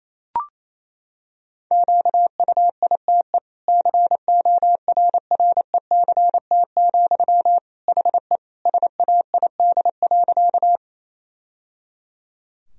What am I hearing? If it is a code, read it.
Morse code, heard as a keyed tone, about 28 wpm, 700 Hz. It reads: QUITE CORRECT, HE SAID.
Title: Sentences from Top 1000 Words - Review Part 2 - 28wpm